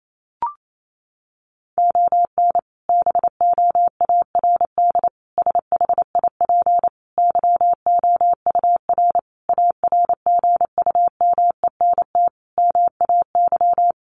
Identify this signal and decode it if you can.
Morse code, heard as a keyed tone, about 28 wpm, 700 Hz. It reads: ON BOARD SHIP YOUR ARGUMENT MAY
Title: Sentences from Top 1400 Words 28wpm